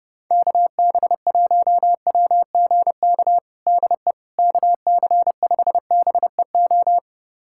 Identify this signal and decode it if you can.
Morse code, heard as a keyed tone, about 30 wpm, 700 Hz. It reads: KB1WGK DE KC5BEO
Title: USA Callsign DE Callsign - 6 Characters - 30wpm